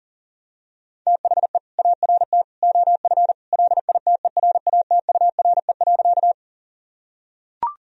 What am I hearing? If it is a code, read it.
Morse code, heard as a keyed tone, about 40 wpm, 700 Hz. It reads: THE ART OF LITERATURE.
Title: Sentences from Top 1000 Words - Review Part 3 - 40wpm